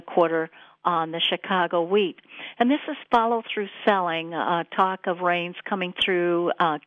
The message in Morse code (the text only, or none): none